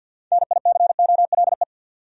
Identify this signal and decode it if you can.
Morse code, heard as a keyed tone, about 48 wpm, 700 Hz. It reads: NICOLE
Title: USA Common Names - 50wpm